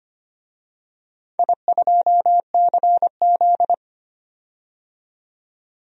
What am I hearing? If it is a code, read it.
Morse code, heard as a keyed tone, about 25 wpm, 690 Hz. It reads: I2CZ
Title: DXCC-100 Call Signs - 4 Characters - 25wpm